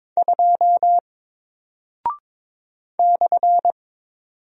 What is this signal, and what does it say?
Morse code, heard as a keyed tone, about 22 wpm, 700 Hz. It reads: 2 /
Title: Single Character (Letter:Number:Punctuation:Prosign) Alt 22wpm